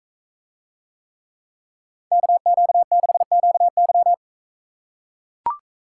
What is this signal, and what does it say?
Morse code, heard as a keyed tone, about 44 wpm, 700 Hz. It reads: KQ6QY